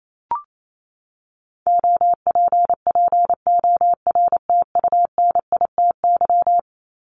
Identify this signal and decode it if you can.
Morse code, heard as a keyed tone, about 28 wpm, 700 Hz. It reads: OPPORTUNITY
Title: Top 751-1000 Words 28wpm